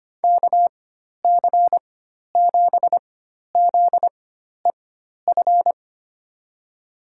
Morse code, heard as a keyed tone, about 25 wpm, spaced as for 12 wpm, 700 Hz. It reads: KC7ZEF
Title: USA Call Signs - 6 Characters - 4x Character Spacing - 25wpm